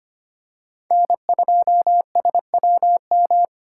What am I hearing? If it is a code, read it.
Morse code, heard as a keyed tone, about 25 wpm, 700 Hz. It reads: N2SWM
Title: USA Call Signs - 5 Characters - 2x Character Spacing - 25wpm